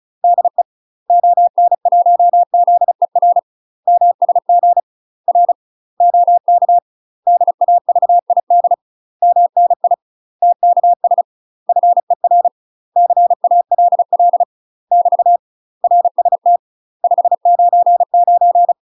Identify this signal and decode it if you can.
Morse code, heard as a keyed tone, about 34 wpm, 700 Hz. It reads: DE ON1ZER MSG R OK DAVID MNI TKS FER CALL <BT> RST 599
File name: UK MASTER QSOs TRAINING / 35WPM / UK MASTER QSOs 0040